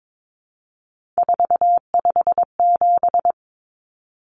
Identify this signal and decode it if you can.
Morse code, heard as a keyed tone, about 22 wpm, 700 Hz. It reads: 457
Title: Numbers - 3-Digit 22wpm